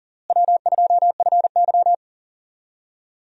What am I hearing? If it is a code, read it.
Morse code, heard as a keyed tone, about 40 wpm, 700 Hz. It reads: W2FY
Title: USA Call Signs - 4 Characters - Repeated 3x - 40wpm